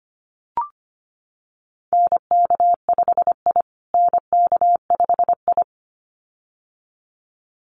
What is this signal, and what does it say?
Morse code, heard as a keyed tone, about 25 wpm, 700 Hz. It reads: NK5I NK5I
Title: USA Call Signs Repeated 2x 25wpm